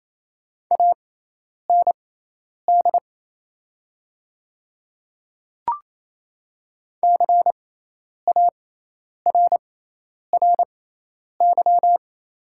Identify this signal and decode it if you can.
Morse code, heard as a keyed tone, about 28 wpm, spaced as for 10 wpm, 700 Hz. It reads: AND CARRY
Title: 6x Character Spacing - Top 300 Words - 28wpm